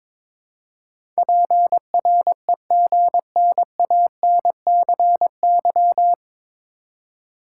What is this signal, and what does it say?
Morse code, heard as a keyed tone, about 22 wpm, 700 Hz. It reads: PREGNANCY
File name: 9 Letter Words - First 3 Letters Given - 22wpm